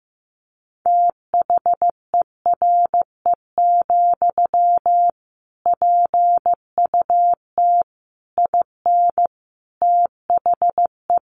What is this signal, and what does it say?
Morse code, heard as a keyed tone, about 15 wpm, 700 Hz. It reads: THERE, PUT IN THE
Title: Sentences from Top 1200 Words 15wpm